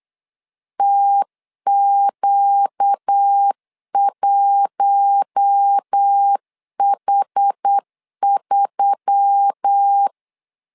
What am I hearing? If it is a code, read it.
Morse code, heard as a keyed tone, about 8 wpm, 800 Hz. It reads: TQ1H3
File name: LEZIONI / 5 LBVC / 5 L 8wpm 2831